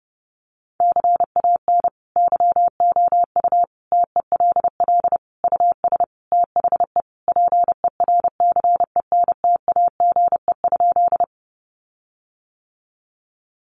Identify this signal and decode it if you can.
Morse code, heard as a keyed tone, about 30 wpm, 700 Hz. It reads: CAN YOU TELL US THE PERCENTAGE?
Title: Sentences from Top 1300 Words 30wpm